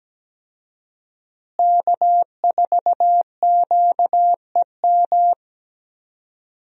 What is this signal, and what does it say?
Morse code, heard as a keyed tone, about 17 wpm, 700 Hz. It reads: K4QEM
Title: USA Call Signs - 5 Characters - Repeated 2x - 17wpm